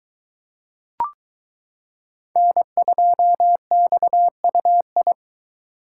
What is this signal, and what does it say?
Morse code, heard as a keyed tone, about 23 wpm, 700 Hz. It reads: N2XUI